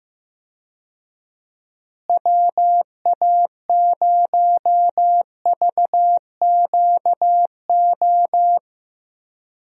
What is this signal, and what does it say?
Morse code, heard as a keyed tone, about 15 wpm, 700 Hz. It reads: WA0VQO
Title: USA Call Signs - 6 Characters - Repeated 3x - 15wpm